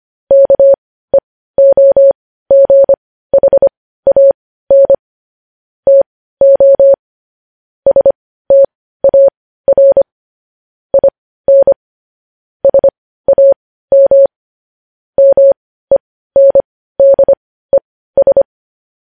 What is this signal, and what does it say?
Morse code, heard as a keyed tone, about 25 wpm, spaced as for 15 wpm, 550 Hz. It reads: KEOGHAN TO STAR IN SAM MENDES
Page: news-headlines-in-morse-code-at-15-wpm